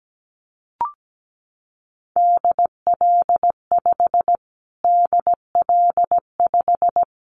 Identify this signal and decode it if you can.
Morse code, heard as a keyed tone, about 17 wpm, 700 Hz. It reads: DL5 DL5
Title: Callsign Prefixes for Top 25 DXCC - Repeated 3x - 17wpm